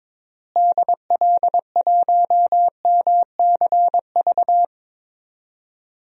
Morse code, heard as a keyed tone, about 22 wpm, 700 Hz. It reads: DL1MCV
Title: DXCC-100 Call Signs - 6 Characters - Repeated 3x - 22wpm